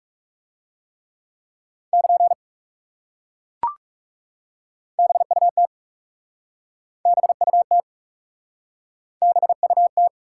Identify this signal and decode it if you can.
Morse code, heard as a keyed tone, about 37 wpm, 700 Hz. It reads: <KN> BUT BUT BUT
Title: Speed-Racing Top 180 QSO Elements 30wpm